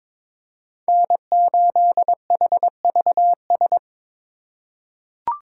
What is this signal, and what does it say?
Morse code, heard as a keyed tone, about 22 wpm, 700 Hz. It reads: N8HVS
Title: USA Call Signs - 5 Characters - 2x Character Spacing - 22wpm